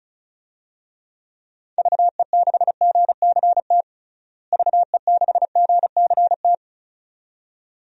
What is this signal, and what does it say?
Morse code, heard as a keyed tone, about 34 wpm, 700 Hz. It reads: VE6GCT VE6GCT
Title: DXCC-100 Call Signs - 6 Characters - Repeated 2x - 35wpm